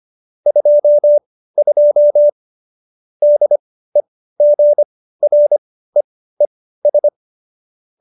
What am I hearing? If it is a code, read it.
Morse code, heard as a keyed tone, about 25 wpm, spaced as for 15 wpm, 590 Hz. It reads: 22 DEGREES
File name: qso222 15